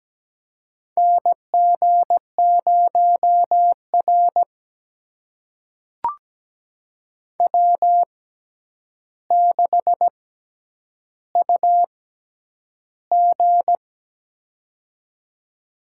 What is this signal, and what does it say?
Morse code, heard as a keyed tone, about 17 wpm, spaced as for 9 wpm, 700 Hz. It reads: NG0R W6UG
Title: USA Call Signs - 6x Character Spacing - 17wpm